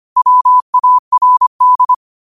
Morse code, heard as a keyed tone, about 25 wpm, 1000 Hz. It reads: WARD
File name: qotdfiles / 25wpm